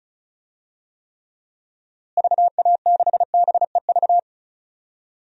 Morse code, heard as a keyed tone, about 34 wpm, 700 Hz. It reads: VA6BEV
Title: DXCC-100 Call Signs - 6 Characters - Repeated 3x - 35wpm